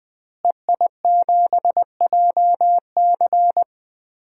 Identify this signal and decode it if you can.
Morse code, heard as a keyed tone, about 20 wpm, 700 Hz. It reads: EI7JC
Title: DXCC-100 Call Signs - 5 Characters - Repeated 2x - 20wpm